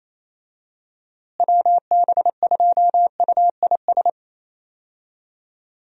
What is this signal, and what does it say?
Morse code, heard as a keyed tone, about 28 wpm, 700 Hz. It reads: WB2UIS
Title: USA Call Signs - 6 Characters - Repeated 2x - 28wpm